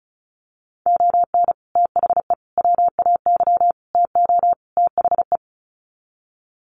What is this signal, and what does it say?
Morse code, heard as a keyed tone, about 34 wpm, 700 Hz. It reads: ON THE WAY TO THE
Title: Sets of 5 Words 35wpm